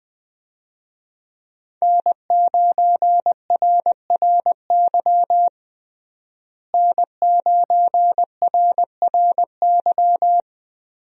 Morse code, heard as a keyed tone, about 20 wpm, 700 Hz. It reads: N9RRY N9RRY
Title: USA Call Signs - 5 Characters - Repeated 2x - 20wpm